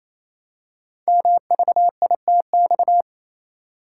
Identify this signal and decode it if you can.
Morse code, heard as a keyed tone, about 27 wpm, 700 Hz. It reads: MVITX
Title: Code Groups - 5 Letters - 28wpm